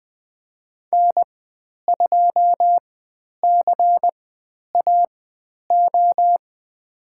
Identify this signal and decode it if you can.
Morse code, heard as a keyed tone, about 20 wpm, spaced as for 10 wpm, 700 Hz. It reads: N2CAO
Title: 102 Lesson 17g - Letter u - Call Signs 20wpm